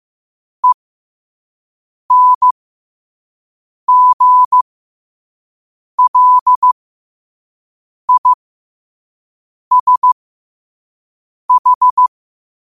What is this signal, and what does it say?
Morse code, heard as a keyed tone, about 15 wpm, spaced as for 5 wpm, 1000 Hz. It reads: ENGLISH
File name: qotdfiles / 5wpm